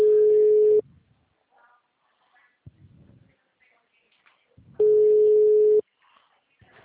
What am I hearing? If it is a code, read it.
Ringback tone (Europe, 425 Hz).